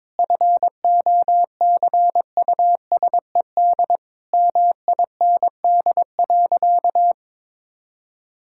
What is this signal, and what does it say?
Morse code, heard as a keyed tone, about 21 wpm, 700 Hz. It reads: FOCUSED MIND.
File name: Difficult Sentences with Hints - 22wpm